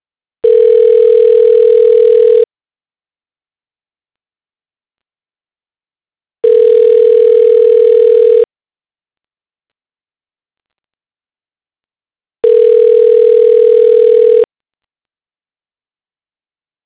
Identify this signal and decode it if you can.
Ringback tone (North America), 440 + 480 Hz.
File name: recordings / RE69df1854d9195e4a9c6d231adb